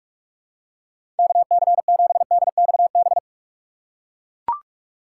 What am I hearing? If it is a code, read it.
Morse code, heard as a keyed tone, about 42 wpm, 700 Hz. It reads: KC7DXB
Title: USA Call Signs - 6 Characters - 4x Character Spacing - 45wpm